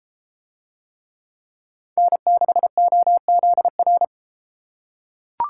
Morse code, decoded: N6OZR

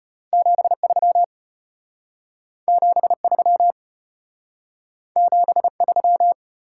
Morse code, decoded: 73 73 73

